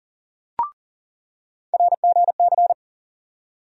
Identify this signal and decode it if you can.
Morse code, heard as a keyed tone, about 40 wpm, 700 Hz. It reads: RGC
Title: USA Call Sign Postfixes - 3 Letters 40wpm